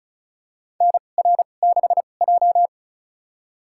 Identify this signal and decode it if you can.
Morse code, heard as a keyed tone, about 36 wpm, spaced as for 25 wpm, 700 Hz. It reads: NR6J
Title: USA Call Signs - 4 Characters - 2x Character Spacing - 35wpm